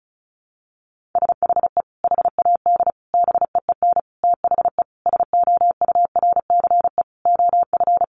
Morse code, heard as a keyed tone, about 34 wpm, 700 Hz. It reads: SHE HAD BEEN THE SOURCE OF
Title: Sentences from Top 1200 Words 35wpm